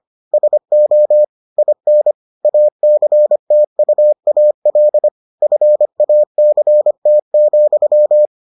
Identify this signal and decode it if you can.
Morse code, heard as a keyed tone, about 25 wpm, 600 Hz. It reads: SO IN ACTUAL FACT,